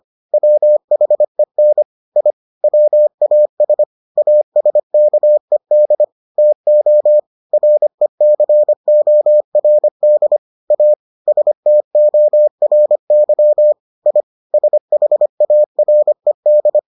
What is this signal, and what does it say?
Morse code, heard as a keyed tone, about 25 wpm, 600 Hz. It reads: WHEN I WAS ASKED TO RECORD A STORY I SHARED